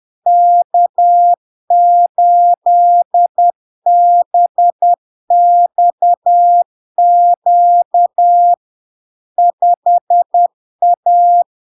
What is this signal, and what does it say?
Morse code, heard as a keyed tone, about 10 wpm, 700 Hz. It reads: K8BXQ 5A
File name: LARK-FD-2022-Log-in-CW-10wpm-700-Hz